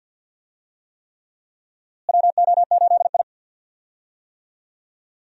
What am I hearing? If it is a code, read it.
Morse code, heard as a keyed tone, about 47 wpm, 700 Hz. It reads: WO8I
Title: USA Call Signs - 4 Characters - Repeated 3x - 50wpm